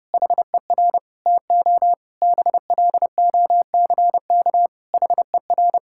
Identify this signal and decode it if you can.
Morse code, heard as a keyed tone, about 30 wpm, 700 Hz. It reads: HER TO BLOCK HER
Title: Sentences from Top 1600 Words 30wpm